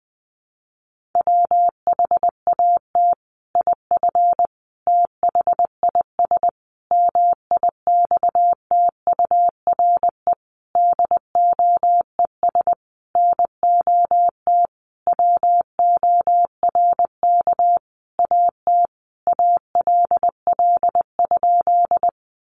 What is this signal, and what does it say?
Morse code, heard as a keyed tone, about 20 wpm, 700 Hz. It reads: WHAT IF THIS MIXTURE DOES NOT WORK AT ALL?
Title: Sentences from Top 1500 Words 20wpm